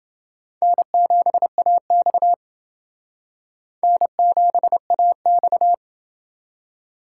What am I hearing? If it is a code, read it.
Morse code, heard as a keyed tone, about 28 wpm, 700 Hz. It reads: N7AX N7AX